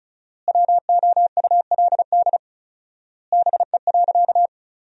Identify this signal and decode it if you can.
Morse code, heard as a keyed tone, about 36 wpm, 700 Hz. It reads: WOULD BE.
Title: 4x Word-Spacing - Sentences from Top 200 Words - 35wpm